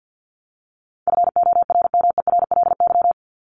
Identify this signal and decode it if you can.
Morse code, heard as a keyed tone, about 50 wpm, 700 Hz. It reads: FORMERLY
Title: LY Suffix - Top 100 Words with Base Words 50wpm